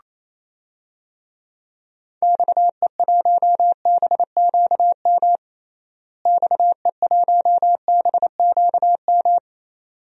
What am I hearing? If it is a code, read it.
Morse code, heard as a keyed tone, about 27 wpm, 700 Hz. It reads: XE1BQM XE1BQM
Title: DXCC-100 Call Signs - 6 Characters - Repeated 2x - 28wpm